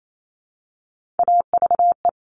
Morse code, heard as a keyed tone, about 28 wpm, 700 Hz. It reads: AVE
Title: Top Three Letter Combinations in Words 28wpm